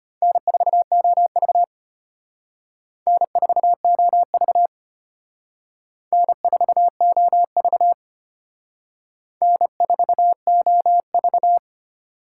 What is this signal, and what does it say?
Morse code, heard as a keyed tone, about 31 wpm, 700 Hz. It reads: N4OV N4OV N4OV N4OV
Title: Speed Racing - USA Call Signs 25wpm